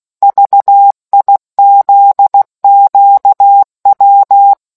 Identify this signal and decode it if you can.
Morse code, heard as a keyed tone, about 16 wpm, 800 Hz. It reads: VIZQW